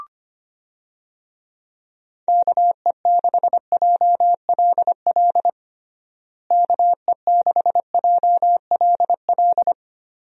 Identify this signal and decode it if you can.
Morse code, heard as a keyed tone, about 25 wpm, 700 Hz. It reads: KE6JLL KE6JLL
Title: USA Call Signs - 6 Characters - Repeated 3x - 25wpm